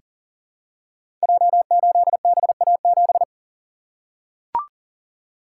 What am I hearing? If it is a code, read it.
Morse code, heard as a keyed tone, about 42 wpm, 700 Hz. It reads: J8BA7
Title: Code Groups - 5 Letter:Numbers - 40wpm